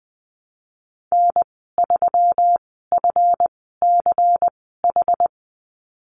Morse code, heard as a keyed tone, about 20 wpm, spaced as for 15 wpm, 700 Hz. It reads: N3FCH
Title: 193 Lesson 31g - Letter v - Call Signs 20wpm